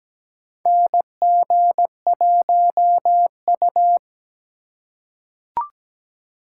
Morse code, heard as a keyed tone, about 17 wpm, 700 Hz. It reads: NG1U
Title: USA Call Signs - 4 Characters - 4x Character Spacing - 17wpm